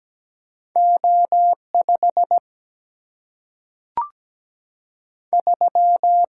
Morse code, heard as a keyed tone, about 17 wpm, 700 Hz. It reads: O5 3
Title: Single-Letter Single-Number 17wpm